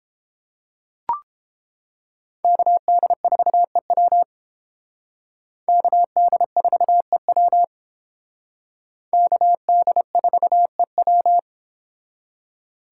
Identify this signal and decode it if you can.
Morse code, heard as a keyed tone, about 29 wpm, 700 Hz. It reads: KD4EW KD4EW KD4EW